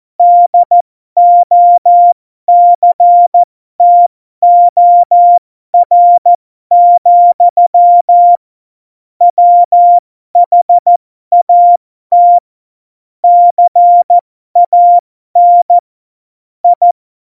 Morse code, heard as a keyed tone, about 14 wpm, spaced as for 12 wpm, 700 Hz. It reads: DOCTOR, WHAT CAN I
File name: Morsekurs / Witze